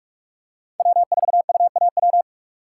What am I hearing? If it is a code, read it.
Morse code, heard as a keyed tone, about 44 wpm, 700 Hz. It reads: W4UAW
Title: USA Call Signs - 5 Characters - 3x Character Spacing - 45wpm